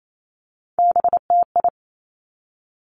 Morse code, heard as a keyed tone, about 28 wpm, 690 Hz. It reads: BTI
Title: Three Challenging Letter-Number Combinations 28wpm